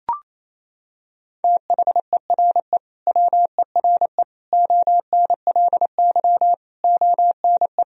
Morse code, heard as a keyed tone, about 28 wpm, 700 Hz. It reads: THERE WERE ONLY ONE
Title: Sentences from Top 1000 Words - Review Part 2 - 28wpm